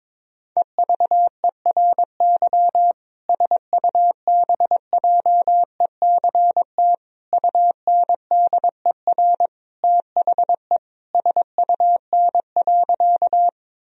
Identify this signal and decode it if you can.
Morse code, heard as a keyed tone, about 22 wpm, 700 Hz. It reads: EVERY SUBJECT UNDER THE SUN.
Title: Sentences from Top 1000 Words - Review Part 1 - 22wpm